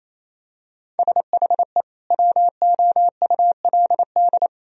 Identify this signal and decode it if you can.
Morse code, heard as a keyed tone, about 28 wpm, 700 Hz. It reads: SHE WOULD
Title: Sentences from Top 1000 Words - Review Part 3 - 28wpm